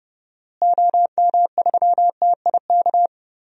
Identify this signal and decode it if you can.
Morse code, heard as a keyed tone, about 30 wpm, 700 Hz. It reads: OM3TIK
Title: DXCC-100 Call Signs - 6 Characters - Repeated 3x - 30wpm